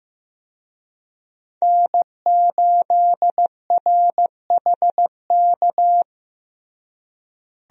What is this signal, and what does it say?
Morse code, heard as a keyed tone, about 16 wpm, 700 Hz. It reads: N8RHK
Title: USA Call Signs - 5 Characters - Repeated 3x - 15wpm